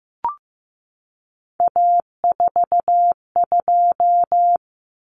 Morse code, heard as a keyed tone, about 15 wpm, 700 Hz. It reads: A42
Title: Common Cut Numbers - 3-Digit 15wpm